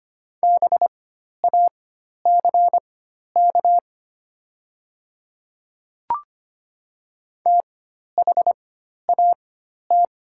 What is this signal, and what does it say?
Morse code, heard as a keyed tone, about 25 wpm, spaced as for 12 wpm, 700 Hz. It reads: BACK THAT
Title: 4x Character Spacing - Top 200 Words - 25wpm